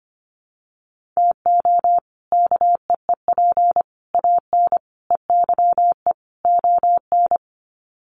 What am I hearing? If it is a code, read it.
Morse code, heard as a keyed tone, about 25 wpm, 700 Hz. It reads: TO KEEP AN EYE ON
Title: Sets of 5 Words Encore 25wpm